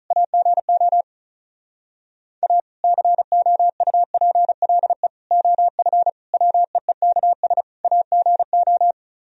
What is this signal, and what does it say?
Morse code, heard as a keyed tone, about 43 wpm, 700 Hz. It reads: AGO A COUPLE OF WEEKS AGO
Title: Speed-Racing - Sets of 5 Words 35wpm